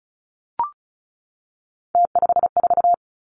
Morse code, heard as a keyed tone, about 36 wpm, 700 Hz. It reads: T54
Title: Three Challenging Letter-Number Combinations 35wpm